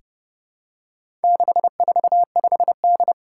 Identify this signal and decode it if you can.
Morse code, heard as a keyed tone, about 30 wpm, 700 Hz. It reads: 645D
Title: Four Challenging Letter-Number Combinations 30wpm